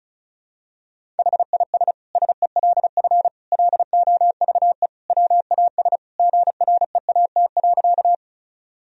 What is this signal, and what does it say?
Morse code, heard as a keyed tone, about 36 wpm, 700 Hz. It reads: HIS SELF LOVE WAS GREAT.
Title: Sentences from Top 1000 Words - Review Part 1 - 35wpm